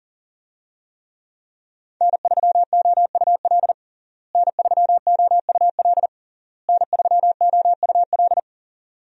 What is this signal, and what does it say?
Morse code, heard as a keyed tone, about 40 wpm, 690 Hz. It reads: N3OUL N3OUL N3OUL
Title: USA Call Signs - 5 Characters - Repeated 3x - 40wpm